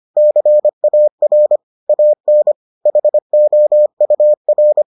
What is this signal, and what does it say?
Morse code, heard as a keyed tone, about 25 wpm, 600 Hz. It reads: CAR AN HOUR